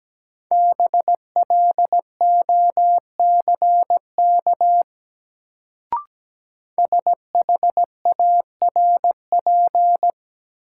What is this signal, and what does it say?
Morse code, heard as a keyed tone, about 17 wpm, 700 Hz. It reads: BLOCK SHARP
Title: Sentences from Top 1600 Words - New Word Drill - 17wpm